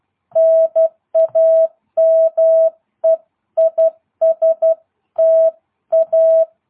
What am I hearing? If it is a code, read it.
Morse code, heard as a keyed tone, about 12 wpm, 650 Hz. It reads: NAME IS TA
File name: jh8cbh / 2022cw